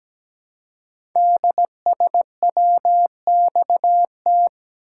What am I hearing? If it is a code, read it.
Morse code, heard as a keyed tone, about 17 wpm, 700 Hz. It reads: DSWXT
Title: Code Groups - 5 Letters - 17wpm